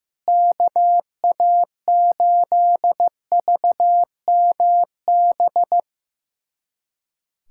Morse code, heard as a keyed tone, about 15 wpm, 700 Hz. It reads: KA8VMB
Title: USA Callsign DE Callsign 15wpm